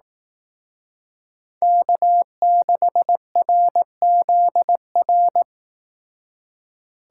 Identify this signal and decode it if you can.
Morse code, heard as a keyed tone, about 18 wpm, 700 Hz. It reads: K6RZR
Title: Speed Racing - USA 1x3 Call Signs 15wpm